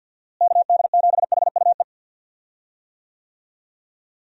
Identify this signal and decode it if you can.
Morse code, heard as a keyed tone, about 46 wpm, 700 Hz. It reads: KD7HUE
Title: USA Call Signs - 6 Characters - Repeated 3x - 50wpm